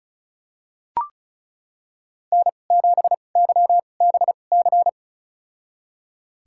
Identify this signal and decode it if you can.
Morse code, heard as a keyed tone, about 34 wpm, spaced as for 25 wpm, 700 Hz. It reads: N7YBC
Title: USA Call Signs - 5 Characters - 2x Character Spacing - 35wpm